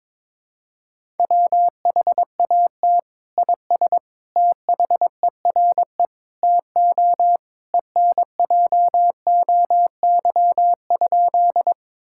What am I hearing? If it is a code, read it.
Morse code, heard as a keyed tone, about 22 wpm, 700 Hz. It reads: WHAT IS THERE TO ENJOY?
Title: Sentences from Top 1000 Words - Review Part 5 - 22wpm